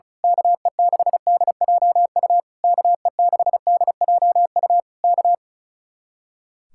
Morse code, heard as a keyed tone, about 32 wpm, 700 Hz. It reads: KE6DJU KE6DJU K
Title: Speed-Racing Common QSO Phrases - 35wpm